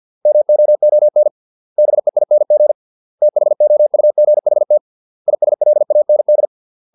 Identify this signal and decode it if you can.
Morse code, heard as a keyed tone, about 50 wpm, 600 Hz. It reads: MOON BEING THOUGHT ISLAND